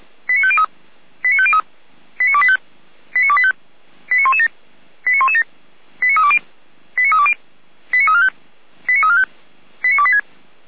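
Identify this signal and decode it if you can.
Five-tone selective call (ZVEI), heard as quick runs of five tones, tones of about 70 ms. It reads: 89503 89503 89285 89285 89117 89117 89320 89320 89345 89345 89286
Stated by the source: Received with an SDR; ZVEI five-tone selective call.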